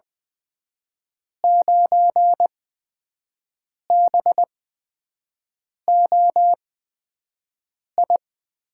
Morse code, heard as a keyed tone, about 20 wpm, spaced as for 5 wpm, 700 Hz. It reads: 9BOI
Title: USA Call Signs - 6 Characters - 8x Character Spacing - 20wpm